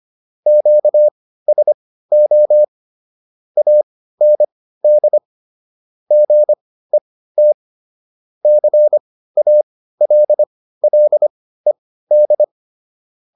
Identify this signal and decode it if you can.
Morse code, heard as a keyed tone, about 25 wpm, spaced as for 15 wpm, 600 Hz. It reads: QSO AND GET CALLED